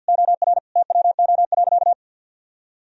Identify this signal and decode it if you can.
Morse code, heard as a keyed tone, about 52 wpm, 700 Hz. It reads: OR TWO.
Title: Sentences from Top 1000 Words - Review Part 2 - 50wpm